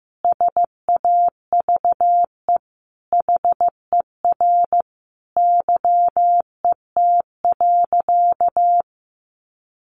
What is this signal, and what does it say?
Morse code, heard as a keyed tone, about 15 wpm, 700 Hz. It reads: SAVE HER YET.
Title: Sentences from Top 1100 Words 15wpm